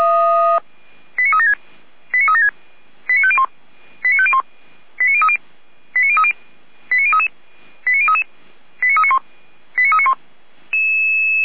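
Five-tone selective call (ZVEI), heard as quick runs of five tones, tones of about 70 ms. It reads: 89376 89376 89501 89501 89030 89030 89033 89033 89391 89391
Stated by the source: Five-tone ZVEI selective call 89058, received on SDR, from Feuerwehr Orlamünde.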